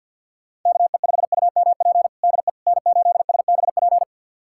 Morse code, heard as a keyed tone, about 51 wpm, 700 Hz. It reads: KE5UMP DE N8SBP
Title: USA Callsign DE Callsign 50wpm